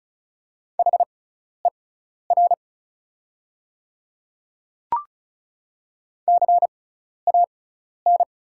Morse code, heard as a keyed tone, about 36 wpm, spaced as for 12 wpm, 700 Hz. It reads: HER CAN
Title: 6x Character Spacing - Top 100 Words - 35wpm